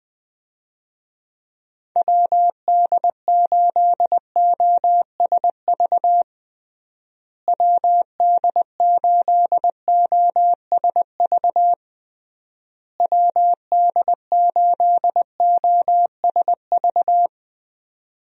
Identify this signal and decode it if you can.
Morse code, heard as a keyed tone, about 20 wpm, 700 Hz. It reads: WD8OSV WD8OSV WD8OSV
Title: USA Call Signs - 6 Characters - Repeated 3x - 20wpm